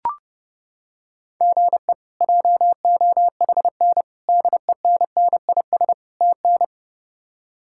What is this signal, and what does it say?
Morse code, heard as a keyed tone, about 32 wpm, 700 Hz. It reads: GE JOHN DENNIS TN
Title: K1USN SST QSO Practice - 30wpm